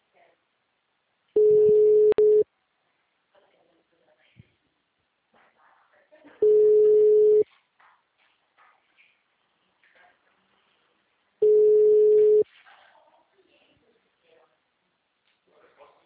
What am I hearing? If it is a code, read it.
Ringback tone (Europe, 425 Hz), with speech.